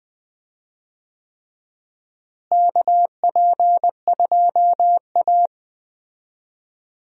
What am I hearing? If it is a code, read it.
Morse code, heard as a keyed tone, about 20 wpm, 700 Hz. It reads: KP2A
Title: DXCC-100 Call Signs - 4 Characters - Repeated 3x - 20wpm